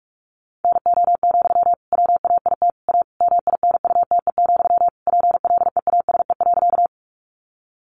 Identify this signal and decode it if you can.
Morse code, heard as a keyed tone, about 44 wpm, 700 Hz. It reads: NO, WAIT A MINUTE, PLEASE.
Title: Sentences from Top 1500 Words 45wpm